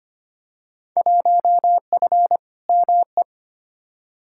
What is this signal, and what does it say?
Morse code, heard as a keyed tone, about 25 wpm, 700 Hz. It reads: 1F ME
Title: Field Day Exchanges Encore - 25wpm